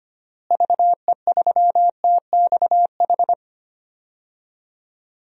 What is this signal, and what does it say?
Morse code, heard as a keyed tone, about 25 wpm, 700 Hz. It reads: VE3TXH
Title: DXCC-100 Call Signs - 6 Characters - Repeated 3x - 25wpm